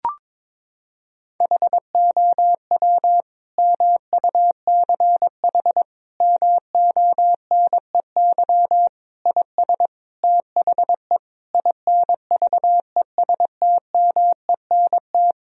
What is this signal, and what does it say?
Morse code, heard as a keyed tone, about 22 wpm, 700 Hz. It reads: HOW MUCH MONEY IS THE INVESTMENT
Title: Sentences from Top 1000 Words - Review Part 4 - 22wpm